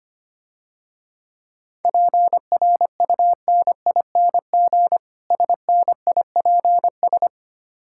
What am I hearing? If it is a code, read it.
Morse code, heard as a keyed tone, about 25 wpm, 700 Hz. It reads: PRUNING SNIPS
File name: practice / 25wpm / Difficult Words - Speed-Racing with Hints - 25wpm